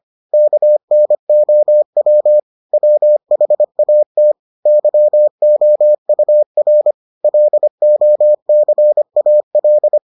Morse code, heard as a keyed tone, about 25 wpm, 600 Hz. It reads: KNOW WHAT YOUR LOCAL